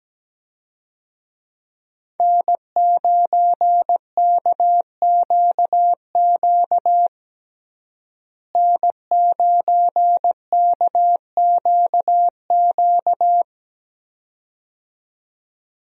Morse code, heard as a keyed tone, about 17 wpm, 700 Hz. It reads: N9KQQ N9KQQ
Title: USA Call Signs - 5 Characters - Repeated 2x - 17wpm